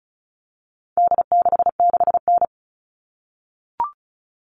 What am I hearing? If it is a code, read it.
Morse code, heard as a keyed tone, about 34 wpm, 700 Hz. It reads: D66N
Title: All Cut Numbers - 4-Digit 35wpm